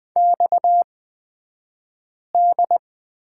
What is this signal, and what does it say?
Morse code, heard as a keyed tone, about 20 wpm, spaced as for 5 wpm, 700 Hz. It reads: XD
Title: 249 Lesson 40d - BK - Two Characters 20wpm